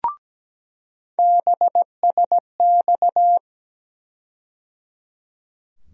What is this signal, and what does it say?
Morse code, heard as a keyed tone, about 17 wpm, 700 Hz. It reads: BSX
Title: USA Call Sign Postfixes - 3 Letters 17wpm